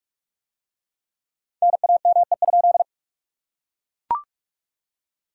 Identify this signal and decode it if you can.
Morse code, heard as a keyed tone, about 40 wpm, 700 Hz. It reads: NAME?